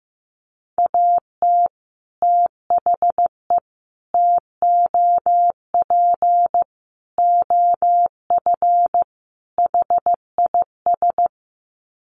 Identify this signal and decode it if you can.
Morse code, heard as a keyed tone, about 15 wpm, 700 Hz. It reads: AT THE TOP OF HIS
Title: Sets of 5 Words Encore 15wpm